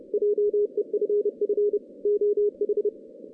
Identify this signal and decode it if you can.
Morse code, heard as a keyed tone, about 29 wpm, 440 Hz. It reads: JEFF OH
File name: 2016 JAN CW NAQP / 2016 Jan naqp k5ot - 0-11